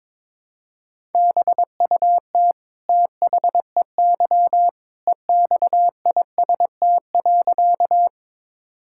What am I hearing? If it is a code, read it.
Morse code, heard as a keyed tone, about 22 wpm, 700 Hz. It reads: BUT THEY EXIST.
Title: Sentences from Top 1000 Words - Review Part 5 - 22wpm